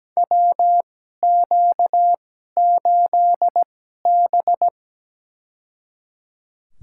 Morse code, heard as a keyed tone, about 17 wpm, spaced as for 12 wpm, 700 Hz. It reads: WQ8B
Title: USA Call Signs - 2x Character Spacing - 17wpm